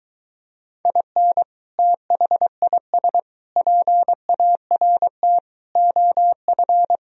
Morse code, heard as a keyed tone, about 23 wpm, 700 Hz. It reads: IN THIS PART OF